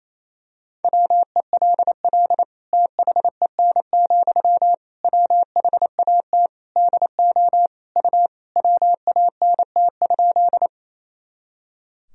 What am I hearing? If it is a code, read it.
Morse code, heard as a keyed tone, about 27 wpm, 700 Hz. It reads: WELL THEN, WHAT DO U WANT?
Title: Sentences from Top 200 Words and Abbreviations 28wpm